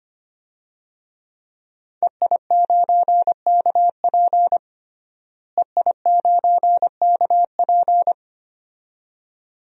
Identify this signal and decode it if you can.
Morse code, heard as a keyed tone, about 25 wpm, 700 Hz. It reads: EI9KP EI9KP
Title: DXCC-100 Call Signs - 5 Characters - Repeated 2x - 25wpm